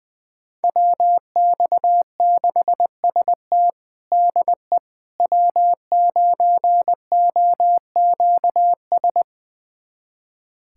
Morse code, heard as a keyed tone, about 20 wpm, 700 Hz. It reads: WX6ST DE W9OQS
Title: USA Callsign DE Callsign - 5 Characters - 20wpm